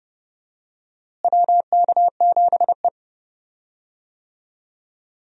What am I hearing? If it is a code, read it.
Morse code, heard as a keyed tone, about 30 wpm, 700 Hz. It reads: WK7E
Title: USA Call Signs - 4 Characters - Repeated 3x - 30wpm